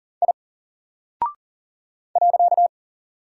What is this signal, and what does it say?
Morse code, heard as a keyed tone, about 39 wpm, 700 Hz. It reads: I .
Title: Single Character (Letter:Number:Punctuation:Prosign) Alt 40wpm